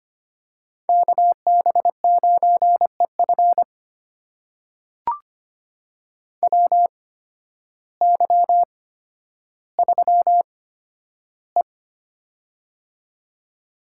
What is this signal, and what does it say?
Morse code, heard as a keyed tone, about 25 wpm, spaced as for 12 wpm, 700 Hz. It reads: KB9EF WY3E
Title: USA Call Signs - 8x Character Spacing - 25wpm